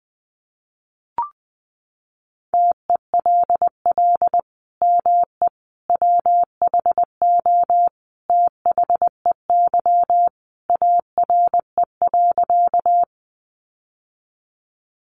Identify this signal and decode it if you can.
Morse code, heard as a keyed tone, about 20 wpm, 700 Hz. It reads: TELL ME WHO THEY ARE.
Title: Sentences from Top 700 Words 20wpm